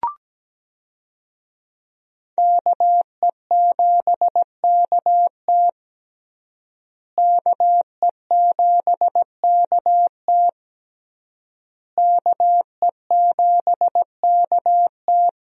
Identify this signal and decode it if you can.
Morse code, heard as a keyed tone, about 17 wpm, 700 Hz. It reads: KE7KT KE7KT KE7KT
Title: USA Call Signs - 5 Characters - Repeated 3x - 17wpm